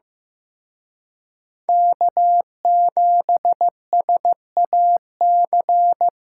Morse code, heard as a keyed tone, about 18 wpm, 700 Hz. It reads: K7SAC
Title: Speed Racing - USA 1x3 Call Signs 15wpm